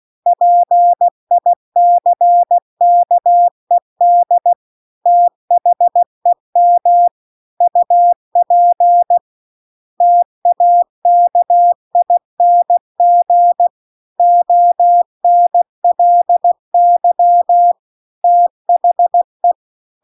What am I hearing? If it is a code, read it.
Morse code, heard as a keyed tone, about 16 wpm, 700 Hz. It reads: PICKED THEM UP TAKING ONLY THE
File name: CW-The-Big-Twoheaded-River-2-16WPM